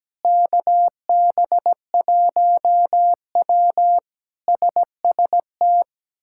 Morse code, heard as a keyed tone, about 17 wpm, 700 Hz. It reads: KB1W SST
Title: K1USN SST QSO Practice - 17wpm